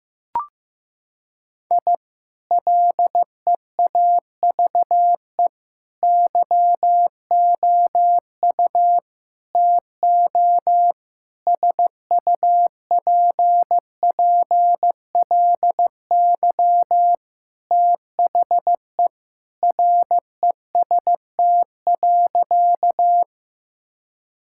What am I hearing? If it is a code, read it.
Morse code, heard as a keyed tone, about 15 wpm, 700 Hz. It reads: I LEAVE YOU TO SUPPLY THE REST.
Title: Sentences from Top 1000 Words - Review Part 5 - 15wpm